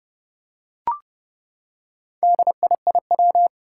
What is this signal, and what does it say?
Morse code, heard as a keyed tone, about 30 wpm, 700 Hz. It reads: DIIW
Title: Four Challenging Letter-Number Combinations 30wpm